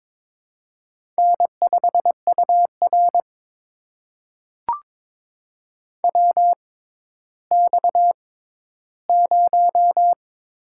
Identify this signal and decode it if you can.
Morse code, heard as a keyed tone, about 22 wpm, spaced as for 13 wpm, 700 Hz. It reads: N5UR WX0
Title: USA Call Signs - 4 Characters - 6x Character Spacing - 22wpm